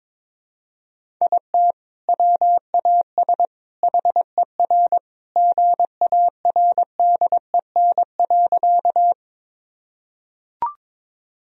Morse code, heard as a keyed tone, about 22 wpm, 700 Hz. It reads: IT WAS HER GARDEN.
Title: Sentences from Top 1000 Words - Review Part 4 - 22wpm